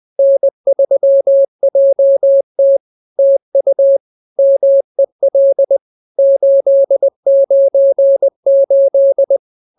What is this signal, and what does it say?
Morse code, heard as a keyed tone, about 20 wpm, 550 Hz. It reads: N3JT TU MEL 898
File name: CWT-202-20